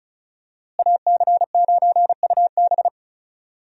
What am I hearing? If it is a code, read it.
Morse code, heard as a keyed tone, about 36 wpm, 700 Hz. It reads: AC9UB
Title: USA Call Signs - 5 Characters - Repeated 2x - 35wpm